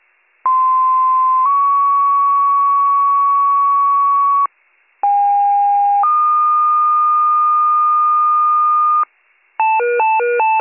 Two-tone page, heard as a one-second tone, then a three-second tone, about 1040 then 1120 Hz, about 810 then 1220 Hz. It is heard from 0.4 to 9.0 s.